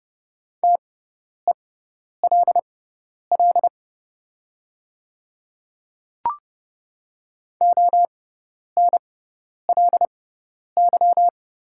Morse code, heard as a keyed tone, about 30 wpm, spaced as for 10 wpm, 700 Hz. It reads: TELL ONLY
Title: 6x Character Spacing - Top 200 Words - 30wpm